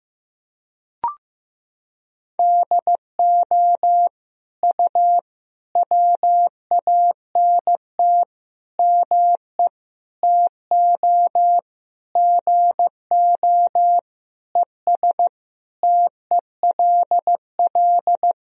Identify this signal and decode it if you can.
Morse code, heard as a keyed tone, about 15 wpm, 700 Hz. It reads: DO U WANT ME TO GO ES TELL
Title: Sentences from Top 200 Words and Abbreviations 15wpm